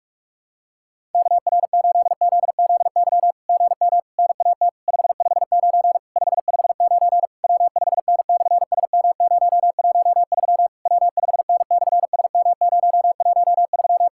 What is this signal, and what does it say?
Morse code, heard as a keyed tone, about 46 wpm, 700 Hz. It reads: KR8ZZY GM NAT 559 559 W5N/SM013 W5N/SM013
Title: SOTA QSO Practice 45wpm